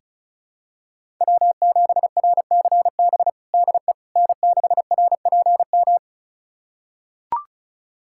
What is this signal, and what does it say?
Morse code, heard as a keyed tone, about 34 wpm, 700 Hz. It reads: W7RCB DE N6RPM
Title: USA Callsign DE Callsign - 5 Characters - 35wpm